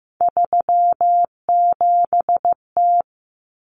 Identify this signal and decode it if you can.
Morse code, heard as a keyed tone, about 15 wpm, 700 Hz. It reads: 37T
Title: Common Cut Numbers - 3-Digit 15wpm